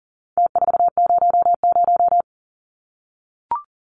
Morse code, heard as a keyed tone, about 40 wpm, 700 Hz. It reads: T400
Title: Numbers - Contest Style - 4-Digit 40wpm